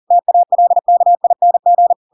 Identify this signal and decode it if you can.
Morse code, heard as a keyed tone, about 40 wpm, 700 Hz. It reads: TALKING
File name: Pinocchio / Pinocchio ch7 40 40 ik1hge